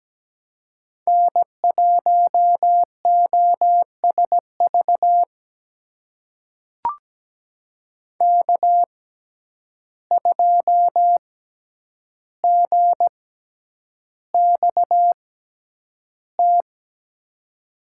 Morse code, heard as a keyed tone, about 17 wpm, spaced as for 9 wpm, 700 Hz. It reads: N1OSV K2GXT
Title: USA Call Signs - 5 Characters - 6x Character Spacing - 17wpm